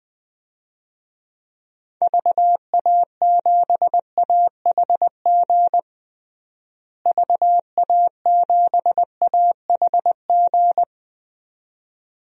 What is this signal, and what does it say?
Morse code, heard as a keyed tone, about 20 wpm, 700 Hz. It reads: VA7AHG VA7AHG
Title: DXCC-100 Call Signs - 6 Characters - Repeated 2x - 20wpm